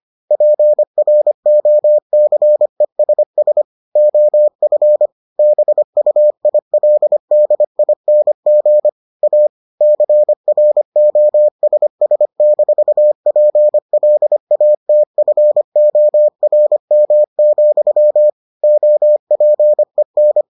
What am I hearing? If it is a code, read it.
Morse code, heard as a keyed tone, about 25 wpm, 600 Hz. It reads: PROCESS OF BUILDING A CROSS-PLATFORM, OPEN